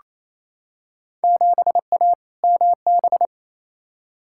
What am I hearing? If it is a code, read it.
Morse code, heard as a keyed tone, about 28 wpm, 700 Hz. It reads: 7A MB
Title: Field Day Exchanges Encore - 28wpm